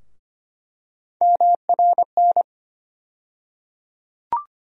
Morse code, heard as a keyed tone, about 25 wpm, 700 Hz. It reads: MRN
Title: USA Call Sign Postfixes - 3 Letters 25wpm